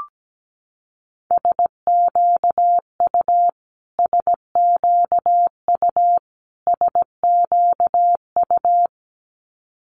Morse code, heard as a keyed tone, about 17 wpm, 700 Hz. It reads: SQU SQU SQU
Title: Multi-letter Phonemes Repeated 3x 17wpm